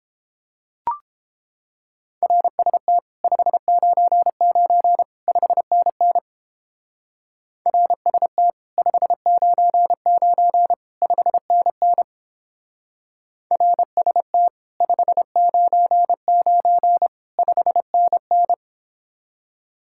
Morse code, heard as a keyed tone, about 29 wpm, 700 Hz. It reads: RST 599 5NN RST 599 5NN RST 599 5NN